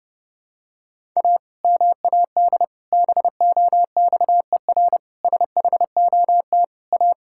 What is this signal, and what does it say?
Morse code, heard as a keyed tone, about 30 wpm, 700 Hz. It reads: A MAD BOXER SHOT A
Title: Pangram Challenge v2 - 30wpm